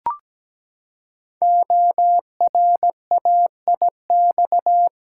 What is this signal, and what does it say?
Morse code, heard as a keyed tone, about 17 wpm, 700 Hz. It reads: ORAIX
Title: Code Groups - 5 Letters - 17wpm